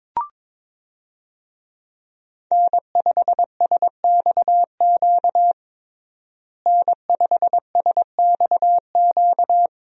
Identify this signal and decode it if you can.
Morse code, heard as a keyed tone, about 22 wpm, 700 Hz. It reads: N5SXQ N5SXQ
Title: USA Call Signs - 5 Characters - Repeated 2x - 22wpm